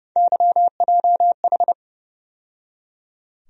Morse code, heard as a keyed tone, about 29 wpm, 700 Hz. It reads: YJH
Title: USA Call Sign Postfixes - 3 Letters 30wpm